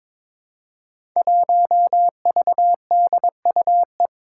Morse code, heard as a keyed tone, about 21 wpm, 700 Hz. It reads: 1VDUE
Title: Five Challenging Letter-Number Combinations 22wpm